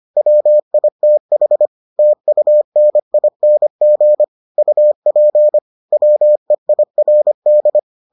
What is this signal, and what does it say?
Morse code, heard as a keyed tone, about 25 wpm, 600 Hz. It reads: WITH TUNING UP WEIRD